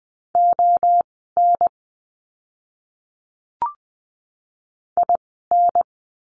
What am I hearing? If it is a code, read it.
Morse code, heard as a keyed tone, about 20 wpm, spaced as for 15 wpm, 700 Hz. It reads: ON IN
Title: 054 Lesson 10f - Letter r - Words 20wpm